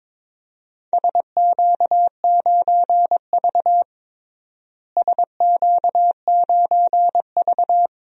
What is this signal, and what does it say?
Morse code, heard as a keyed tone, about 21 wpm, 700 Hz. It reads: SQ9V SQ9V
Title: DXCC-100 Call Signs - 4 Characters - Repeated 2x - 22wpm